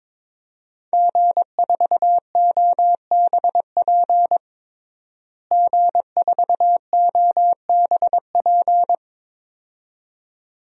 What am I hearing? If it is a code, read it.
Morse code, heard as a keyed tone, about 22 wpm, 700 Hz. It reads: G4OBP G4OBP
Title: DXCC-100 Call Signs - 5 Characters - Repeated 2x - 22wpm